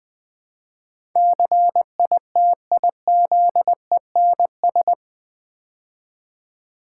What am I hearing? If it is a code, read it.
Morse code, heard as a keyed tone, about 20 wpm, 700 Hz. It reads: CITIZENS
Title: Build A Word - 4 Letters At A Time - 11-Letter Words - 20wpm